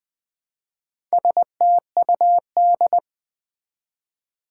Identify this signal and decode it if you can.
Morse code, heard as a keyed tone, about 20 wpm, 700 Hz. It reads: STUD
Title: Build A Word - 1 Letter At A Time - 5-Letter Words - 20wpm